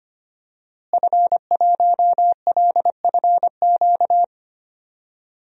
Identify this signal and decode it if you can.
Morse code, heard as a keyed tone, about 25 wpm, 700 Hz. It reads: F1LFQ
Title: DXCC-100 Call Signs - 5 Characters - Repeated 3x - 25wpm